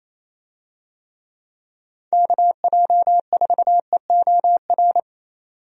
Morse code, heard as a keyed tone, about 28 wpm, 700 Hz. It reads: KJ4EOR
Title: USA Call Signs - 6 Characters - 28wpm